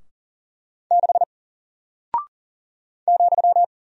Morse code, heard as a keyed tone, about 42 wpm, 700 Hz. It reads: 6 ,
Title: Single Character (Letter:Number:Punctuation:Prosign) 40wpm